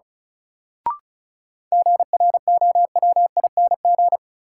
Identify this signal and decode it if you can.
Morse code, heard as a keyed tone, about 36 wpm, 700 Hz. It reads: GROWING
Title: Sets of 5 Words Encore - New Word Drill - 35wpm